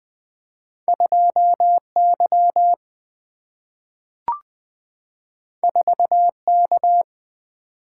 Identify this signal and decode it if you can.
Morse code, heard as a keyed tone, about 20 wpm, 700 Hz. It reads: 2Y 4K
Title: Single-Letter Single-Number 20wpm